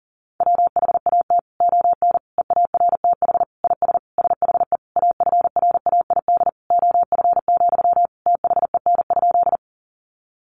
Morse code, heard as a keyed tone, about 40 wpm, 700 Hz. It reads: WHAT ON EARTH IS SHE AFRAID OF, THEN?
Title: Sentences from Top 1500 Words 40wpm